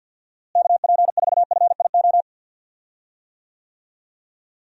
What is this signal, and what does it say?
Morse code, heard as a keyed tone, about 52 wpm, 700 Hz. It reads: KP4FIO
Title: USA Call Signs - 6 Characters - Repeated 3x - 50wpm